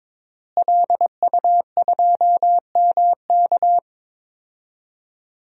Morse code, heard as a keyed tone, about 22 wpm, 700 Hz. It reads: LU2MK
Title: DXCC-100 Call Signs - 5 Characters - Repeated 3x - 22wpm